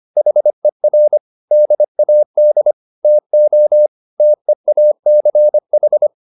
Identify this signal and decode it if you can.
Morse code, heard as a keyed tone, about 25 wpm, 600 Hz. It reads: HER DAD TO TEACH